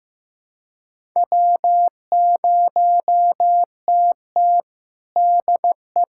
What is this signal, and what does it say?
Morse code, heard as a keyed tone, about 15 wpm, 700 Hz. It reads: W0TT DE
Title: USA Callsign DE Callsign - 4 Characters - 15wpm